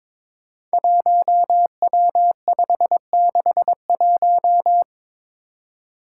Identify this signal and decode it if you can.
Morse code, heard as a keyed tone, about 22 wpm, 700 Hz. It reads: 1W561
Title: Five Challenging Letter-Number Combinations 22wpm